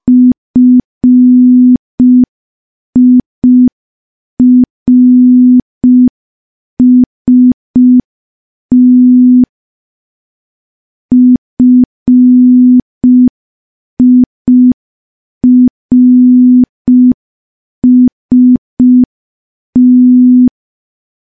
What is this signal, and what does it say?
Morse code, heard as a keyed tone, about 5 wpm, 250 Hz. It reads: FIRST FIRST
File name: morse / morse-training-05WPM